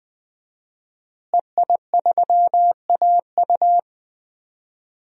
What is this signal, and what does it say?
Morse code, heard as a keyed tone, about 20 wpm, 700 Hz. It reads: EI3AU
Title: DXCC-100 Call Signs - 5 Characters - Repeated 2x - 20wpm